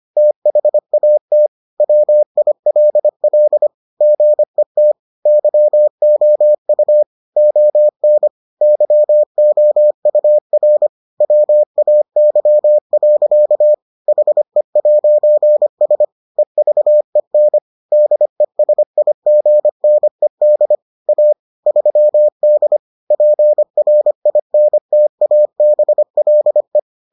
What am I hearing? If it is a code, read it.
Morse code, heard as a keyed tone, about 25 wpm, 600 Hz. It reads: THAT WILL GET YOU ON YOUR WAY. HE'S EVEN DESIGNED A 3D PRINTABLE